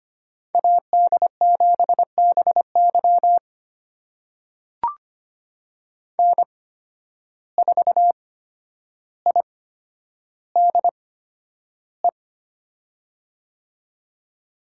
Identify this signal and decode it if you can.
Morse code, heard as a keyed tone, about 25 wpm, spaced as for 11 wpm, 700 Hz. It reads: AD7BY N4IDE
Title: USA Call Signs - 5 Characters - 8x Character Spacing - 25wpm